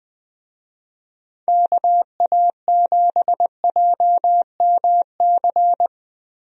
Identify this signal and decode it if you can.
Morse code, heard as a keyed tone, about 20 wpm, 700 Hz. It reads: KA7JMC
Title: USA Call Signs - 6 Characters - Repeated 2x - 20wpm